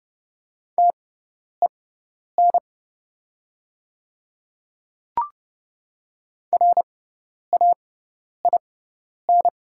Morse code, heard as a keyed tone, about 30 wpm, spaced as for 10 wpm, 700 Hz. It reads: TEN RAIN
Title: 6x Character Spacing - Top 500 Words - 30wpm